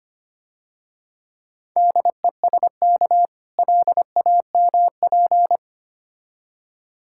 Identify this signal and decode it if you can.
Morse code, heard as a keyed tone, about 26 wpm, 700 Hz. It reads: DESK LAMP
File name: practice / 25wpm / Difficult Words - Speed-Racing with Hints - 25wpm